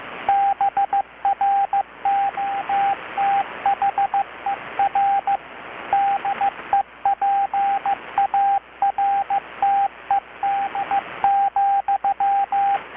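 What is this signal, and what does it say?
Morse code, heard as a keyed tone, about 15 wpm, 780 Hz, with radio noise. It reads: BROTHER DEPARTED,
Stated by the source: Morse from KPH's Night of Nights XX maritime radiotelegraphy event, received in Chiapas, Mexico about 3,800 km away on a Kenwood R-600 with a grounded 10 m long-wire antenna.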